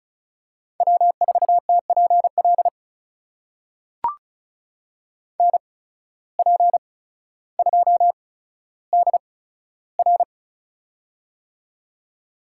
Morse code, heard as a keyed tone, about 36 wpm, spaced as for 15 wpm, 700 Hz. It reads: W4TPL NP2DR
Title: USA Call Signs - 5 Characters - 8x Character Spacing - 35wpm